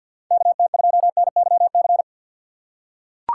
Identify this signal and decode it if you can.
Morse code, heard as a keyed tone, about 46 wpm, 700 Hz. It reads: KT2NYC
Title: USA Call Signs - 6 Characters - 3x Character Spacing - 50wpm